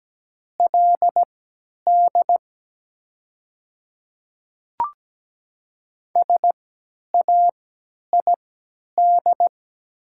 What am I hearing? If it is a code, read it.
Morse code, heard as a keyed tone, about 17 wpm, spaced as for 10 wpm, 700 Hz. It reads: LD SAID
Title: 3x Character Spacing - Top 100 Words - 17wpm